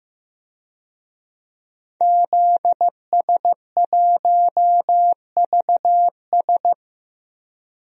Morse code, heard as a keyed tone, about 15 wpm, 700 Hz. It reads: ZS1VS
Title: DXCC-100 Call Signs - 5 Characters - 15wpm